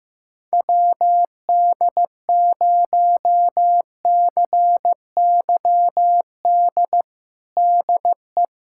Morse code, heard as a keyed tone, about 15 wpm, 700 Hz. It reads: WD0CYD DE
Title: USA Callsign DE Callsign - 6 Characters - 15wpm